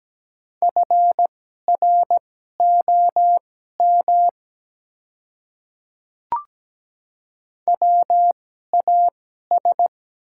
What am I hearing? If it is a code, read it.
Morse code, heard as a keyed tone, about 17 wpm, spaced as for 12 wpm, 700 Hz. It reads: FROM WAS
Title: 2x Character Spacing - Top 100 Words - 17wpm